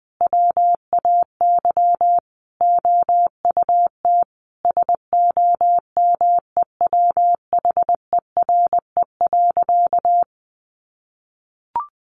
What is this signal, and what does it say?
Morse code, heard as a keyed tone, about 20 wpm, 700 Hz. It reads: WAY OUT SOMEWHERE.
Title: Sentences from Top 1400 Words 20wpm